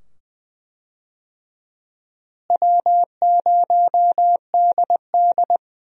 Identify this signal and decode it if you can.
Morse code, heard as a keyed tone, about 20 wpm, 700 Hz. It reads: W0DD
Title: USA Call Signs - 4 Characters - Repeated 2x - 20wpm